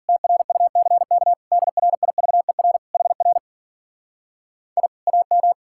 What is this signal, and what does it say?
Morse code, heard as a keyed tone, about 51 wpm, 700 Hz. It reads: TRUCK DRIVER HR I AM